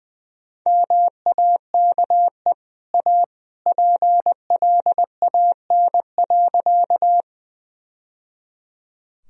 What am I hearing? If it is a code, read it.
Morse code, heard as a keyed tone, about 20 wpm, 700 Hz. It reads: MAKE A PLAN.
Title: Sentences from Top 1000 Words - Review Part 2 - 20wpm